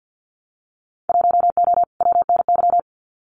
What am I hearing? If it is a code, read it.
Morse code, heard as a keyed tone, about 50 wpm, 700 Hz. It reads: 1O WNY
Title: Winter Field Day Exchanges 50wpm